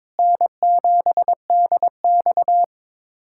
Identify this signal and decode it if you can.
Morse code, heard as a keyed tone, about 22 wpm, 700 Hz. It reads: N7DX
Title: USA Callsign DE Callsign - 4 Characters - 22wpm